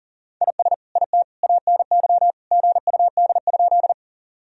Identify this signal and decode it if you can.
Morse code, heard as a keyed tone, about 40 wpm, 700 Hz. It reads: IS IT ANY GUD?
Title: Sentences from Top 200 Words and Abbreviations 40wpm